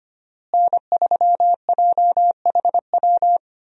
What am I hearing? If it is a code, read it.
Morse code, heard as a keyed tone, about 25 wpm, 700 Hz. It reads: N3JHW
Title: USA Call Signs - 5 Characters - Repeated 2x - 25wpm